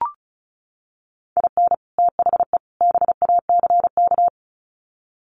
Morse code, heard as a keyed tone, about 34 wpm, 700 Hz. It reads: IN THE BACK
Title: Sets of 3 Words 35wpm